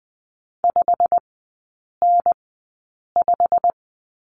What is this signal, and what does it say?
Morse code, heard as a keyed tone, about 20 wpm, spaced as for 8 wpm, 690 Hz. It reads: 5N5
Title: 145 Lesson 24e - Letter y - Three Characters 20wpm